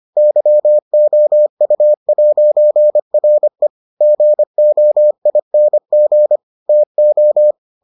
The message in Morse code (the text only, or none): YOU'RE GOING TO